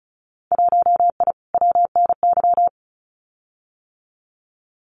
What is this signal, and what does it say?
Morse code, heard as a keyed tone, about 36 wpm, 700 Hz. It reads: 1I WNY
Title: Winter Field Day Exchanges Encore 35wpm